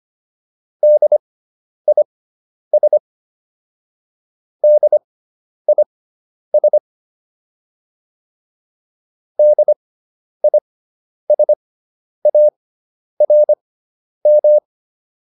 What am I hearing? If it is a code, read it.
Morse code, heard as a keyed tone, about 25 wpm, spaced as for 10 wpm, 600 Hz. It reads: DIS DIS DISARM